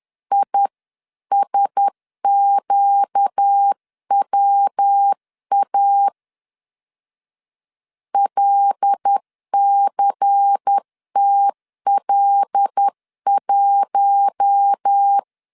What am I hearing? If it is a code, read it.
Morse code, heard as a keyed tone, about 10 wpm, 800 Hz. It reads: ISQWA LCTL1
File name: CW / DET / 1 6 1500C S C 10wpm 4252